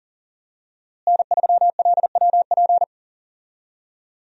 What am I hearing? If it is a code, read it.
Morse code, heard as a keyed tone, about 40 wpm, 700 Hz. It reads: N3LWP
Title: USA Call Signs - 5 Characters - Repeated 2x - 40wpm